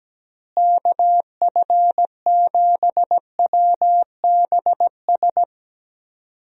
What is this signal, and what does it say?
Morse code, heard as a keyed tone, about 17 wpm, 700 Hz. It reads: KF7WBS
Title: USA Call Signs - 6 Characters - Repeated 3x - 17wpm